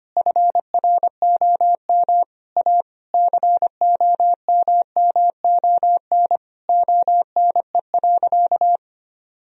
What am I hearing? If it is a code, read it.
Morse code, heard as a keyed tone, about 25 wpm, 700 Hz. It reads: FROM A COMMON ONE.
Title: Sentences from Top 1000 Words - Review Part 4 - 25wpm